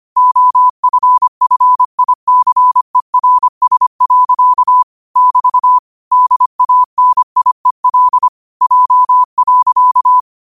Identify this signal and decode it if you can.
Morse code, heard as a keyed tone, about 25 wpm, 1000 Hz. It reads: OFFICERS. <BT> DANIEL J.
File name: qotdfiles / 25wpm